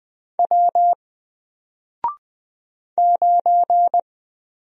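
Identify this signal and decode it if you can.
Morse code, heard as a keyed tone, about 20 wpm, 700 Hz. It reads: W 9
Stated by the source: Single letter and number Morse drill at 20 wpm, from N4LCD.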